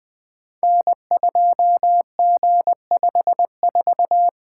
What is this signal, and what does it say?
Morse code, heard as a keyed tone, about 20 wpm, 700 Hz. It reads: N2G54
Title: Code Groups - 5 Letter:Numbers - 20wpm